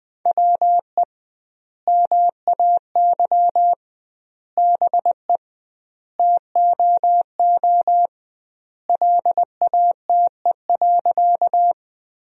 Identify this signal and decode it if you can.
Morse code, heard as a keyed tone, about 21 wpm, 700 Hz. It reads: WE MAY BE TOO LATE.
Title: 2x Word-Spacing - Sentences from Top 300 Words - 20wpm